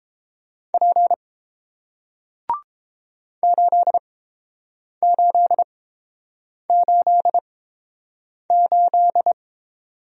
Morse code, heard as a keyed tone, about 30 wpm, spaced as for 7 wpm, 700 Hz. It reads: P 8888